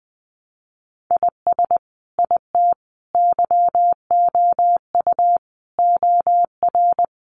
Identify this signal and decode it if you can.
Morse code, heard as a keyed tone, about 20 wpm, 700 Hz. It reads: IS IT YOU OR
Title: Sentences from Top 200 Words 20wpm